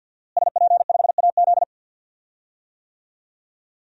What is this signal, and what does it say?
Morse code, heard as a keyed tone, about 53 wpm, 700 Hz. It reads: SP5AZ
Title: DXCC-100 Call Signs - 5 Characters - Repeated 3x - 50wpm